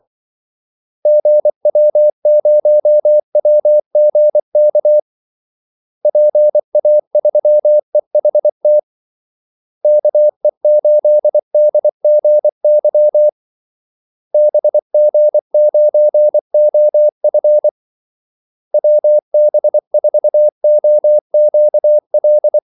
Morse code, heard as a keyed tone, about 24 wpm, 600 Hz. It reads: GW0WGK PA3EHT KE8DGY BG9OF WB4OQL
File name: Callsigns-24wpm